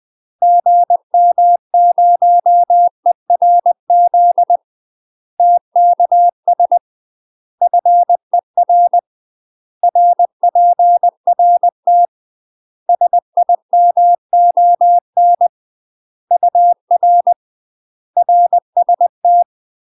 Morse code, heard as a keyed tone, about 20 wpm, 700 Hz. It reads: GM0ERZ TKS FER RPRT SIMON UR RST
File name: UK MASTER QSOs TRAINING / 20WPM / UK MASTER QSOs 0066